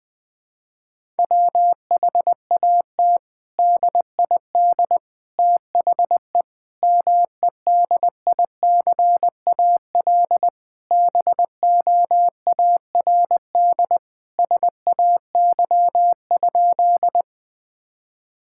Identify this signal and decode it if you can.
Morse code, heard as a keyed tone, about 20 wpm, 700 Hz. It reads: WHAT DID THE MEDICAL BOARD SAY?
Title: Sentences from Top 1000 Words - Review Part 5 - 20wpm